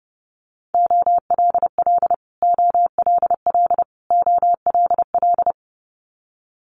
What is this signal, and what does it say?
Morse code, heard as a keyed tone, about 30 wpm, 700 Hz. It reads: OLL OLL OLL
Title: Multi-letter Phonemes Repeated 3x 30wpm